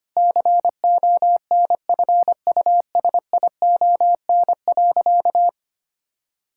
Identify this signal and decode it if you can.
Morse code, heard as a keyed tone, about 25 wpm, 700 Hz. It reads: CONFUSION.
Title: Sentences from Top 1600 Words 25wpm